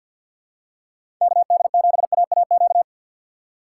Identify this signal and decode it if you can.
Morse code, heard as a keyed tone, about 52 wpm, 700 Hz. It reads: KD7AAQ
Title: USA Call Signs - 6 Characters - 4x Character Spacing - 50wpm